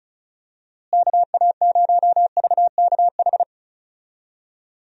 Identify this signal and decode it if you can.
Morse code, heard as a keyed tone, about 34 wpm, 700 Hz. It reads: KA0VKH
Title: USA Call Signs - 6 Characters - Repeated 3x - 35wpm